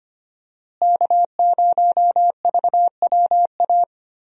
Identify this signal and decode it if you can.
Morse code, heard as a keyed tone, about 25 wpm, 700 Hz. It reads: K0VWA
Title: USA Call Signs - 5 Characters - Repeated 3x - 25wpm